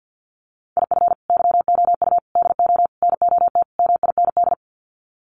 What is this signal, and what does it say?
Morse code, heard as a keyed tone, about 50 wpm, 700 Hz. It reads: IF YOU DO NOT MIND
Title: Sets of 5 Words 50wpm